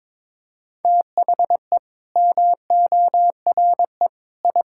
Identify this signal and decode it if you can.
Morse code, heard as a keyed tone, about 22 wpm, 700 Hz. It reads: THE MORE I
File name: practice / 22wpm / Sets of 8 Words - 22wpm